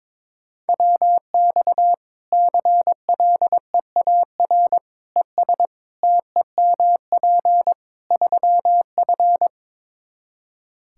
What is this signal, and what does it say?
Morse code, heard as a keyed tone, about 22 wpm, 700 Hz. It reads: WX CLEAR ES TEMP 3F
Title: Practice QSO - Exchange 2 - Info - 22wpm